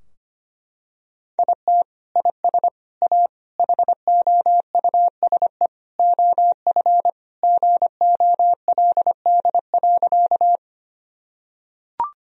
Morse code, heard as a keyed tone, about 25 wpm, 700 Hz. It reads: IT IS A HOUSE OF GOLD.
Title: Sentences from Top 1000 Words - Review Part 3 - 25wpm